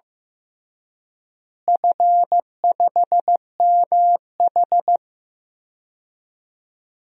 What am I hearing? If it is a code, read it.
Morse code, heard as a keyed tone, about 15 wpm, 700 Hz. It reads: F5MH
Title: DXCC-100 Call Signs - 4 Characters - Repeated 3x - 15wpm